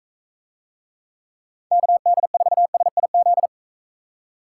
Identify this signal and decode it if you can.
Morse code, heard as a keyed tone, about 42 wpm, 700 Hz. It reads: KD4SIZ